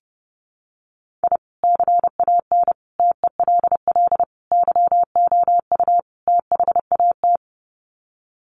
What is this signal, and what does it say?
Morse code, heard as a keyed tone, about 30 wpm, 700 Hz. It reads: I CAN TELL YOU THAT
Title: Sets of 5 Words 30wpm